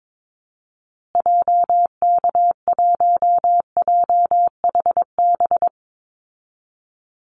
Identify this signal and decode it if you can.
Morse code, heard as a keyed tone, about 22 wpm, 700 Hz. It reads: JK1JHB
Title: Call Signs Repeated 2x - 22wpm